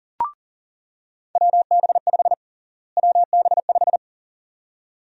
Morse code, heard as a keyed tone, about 40 wpm, 700 Hz. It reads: WB5 WB5
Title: USA Call Sign Prefixes - Repeated 2x - 40wpm